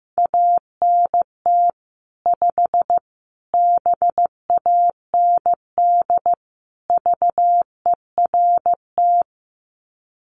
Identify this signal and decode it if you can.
Morse code, heard as a keyed tone, about 15 wpm, 700 Hz. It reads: ANT 5 BAND VERT
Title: Common QSO Phrases 15wpm